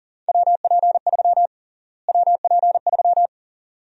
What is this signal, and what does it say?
Morse code, heard as a keyed tone, about 40 wpm, 700 Hz. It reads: WP3 WP3
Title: USA Call Sign Prefixes - Repeated 2x - 40wpm